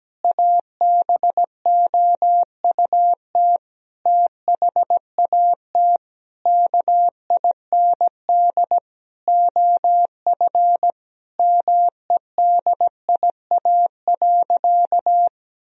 Morse code, heard as a keyed tone, about 17 wpm, 700 Hz. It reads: ABOUT THAT KIND OF MEDIA.
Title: Sentences from Top 1000 Words - Review Part 4 - 17wpm